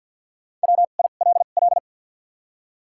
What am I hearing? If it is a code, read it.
Morse code, heard as a keyed tone, about 54 wpm, spaced as for 37 wpm, 700 Hz. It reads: WILL
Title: 2x Character Spacing - Top 200 Words - 50wpm